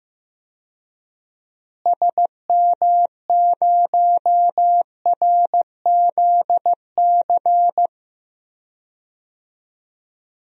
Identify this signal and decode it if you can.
Morse code, heard as a keyed tone, about 15 wpm, 700 Hz. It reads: SM0RZC
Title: DXCC-100 Call Signs - 6 Characters - Repeated 2x - 15wpm